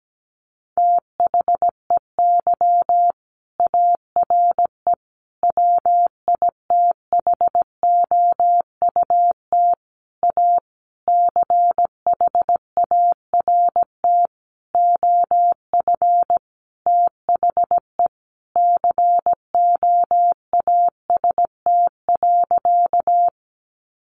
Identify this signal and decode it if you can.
Morse code, heard as a keyed tone, about 17 wpm, 700 Hz. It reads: THEY ARE WITHOUT A CHART OF THE COAST.
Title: Sentences from Top 1300 Words 17wpm